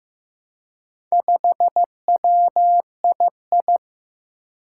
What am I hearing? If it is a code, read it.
Morse code, heard as a keyed tone, about 15 wpm, 700 Hz. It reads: 5WII
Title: Four Challenging Letter-Number Combinations 15wpm